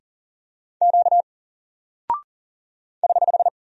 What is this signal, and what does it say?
Morse code, heard as a keyed tone, about 39 wpm, 700 Hz. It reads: Q <HH>
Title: Single Character (Letter:Number:Punctuation:Prosign) 40wpm